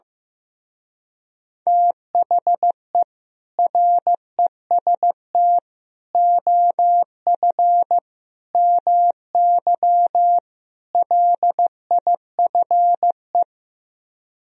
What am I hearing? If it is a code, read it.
Morse code, heard as a keyed tone, about 15 wpm, 700 Hz. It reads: THE REST OF MY LIFE
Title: Speed-Racing - Sets of 5 Words 15wpm